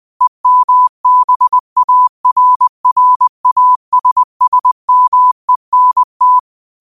Morse code, heard as a keyed tone, about 20 wpm, 1000 Hz. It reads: EMBARRASSMENT
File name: qotdfiles / 20wpm